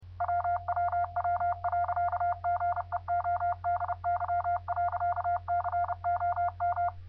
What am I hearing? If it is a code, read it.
Morse code, heard as a keyed tone, about 30 wpm, 700 Hz. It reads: WWW.GEODY.COM